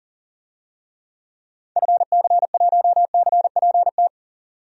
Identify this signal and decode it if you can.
Morse code, heard as a keyed tone, about 40 wpm, 700 Hz. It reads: FC1CPT
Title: DXCC-100 Call Signs - 6 Characters - 40wpm